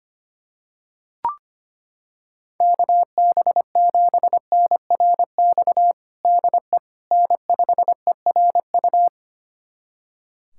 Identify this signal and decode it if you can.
Morse code, heard as a keyed tone, about 25 wpm, 700 Hz. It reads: KB7NRX DE N5ERU
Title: USA Callsign DE Callsign 25wpm